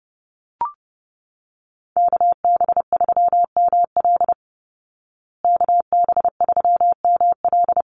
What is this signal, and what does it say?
Morse code, heard as a keyed tone, about 30 wpm, 700 Hz. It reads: KB3ML KB3ML
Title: Call Signs Repeated 3x - 30wpm